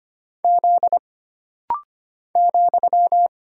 Morse code, heard as a keyed tone, about 25 wpm, 700 Hz. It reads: Z ,
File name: Single Character (Letter:Number:Punctuation:Prosign) - Rapid-Fire - 25wpm